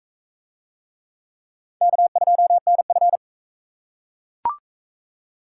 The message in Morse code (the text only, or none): K2NF